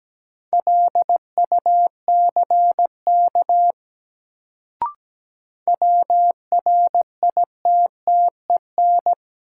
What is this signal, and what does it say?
Morse code, heard as a keyed tone, about 17 wpm, 700 Hz. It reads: LUCK WRITTEN
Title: Sentences from Top 1500 Words - New Word Drill - 17wpm